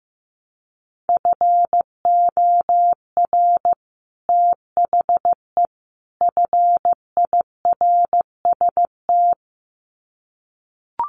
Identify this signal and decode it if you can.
Morse code, heard as a keyed tone, about 15 wpm, 700 Hz. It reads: FOR THE FIRST
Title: Sets of 3 Words 15wpm